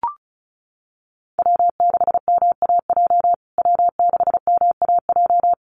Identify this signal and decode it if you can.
Morse code, heard as a keyed tone, about 34 wpm, 700 Hz. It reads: W6MAJ W6MAJ
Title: USA Call Signs Repeated 3x 35wpm